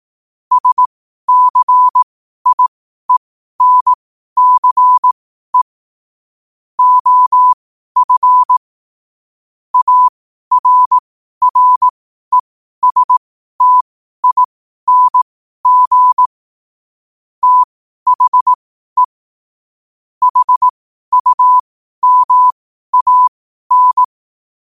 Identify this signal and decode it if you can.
Morse code, heard as a keyed tone, about 18 wpm, spaced as for 12 wpm, 1000 Hz. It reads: SCIENCE OF ARRESTING THE HUMAN
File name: qotdfiles / 12wpm